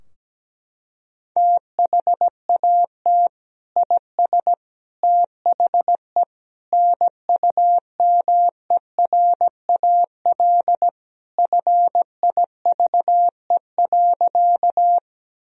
Morse code, heard as a keyed tone, about 17 wpm, 700 Hz. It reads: THAT IS THE NUMERAL FIVE.
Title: Sentences from Top 1000 Words - Review Part 2 - 17wpm